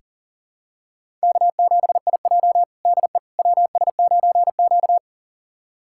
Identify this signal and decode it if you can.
Morse code, heard as a keyed tone, about 37 wpm, 700 Hz. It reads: K7IJ DE WS9Q
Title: USA Callsign DE Callsign - 4 Characters - 40wpm